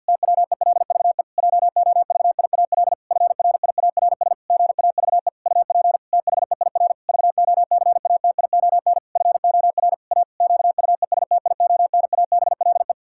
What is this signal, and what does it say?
Morse code, heard as a keyed tone, about 50 wpm, 700 Hz. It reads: TWELVE JOVIAL FRIARS GAVE UP THEIR VOCATION FOR A QUESTIONABLE
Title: Pangram Challenge v2 - 50wpm